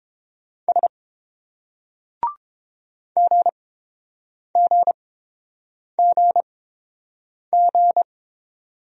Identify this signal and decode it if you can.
Morse code, heard as a keyed tone, about 28 wpm, spaced as for 7 wpm, 700 Hz. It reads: S GGGG